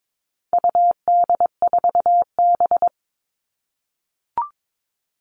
Morse code, heard as a keyed tone, about 22 wpm, 700 Hz. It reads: UD4B
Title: All Cut Numbers - 4-Digit 22wpm